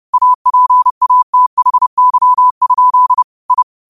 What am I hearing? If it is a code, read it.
Morse code, heard as a keyed tone, about 30 wpm, 1000 Hz. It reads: APATHY? I
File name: qotdfiles / 30wpm